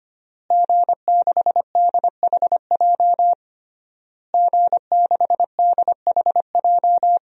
Morse code, heard as a keyed tone, about 25 wpm, 700 Hz. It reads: G6DHJ G6DHJ
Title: DXCC-100 Call Signs - 5 Characters - Repeated 3x - 25wpm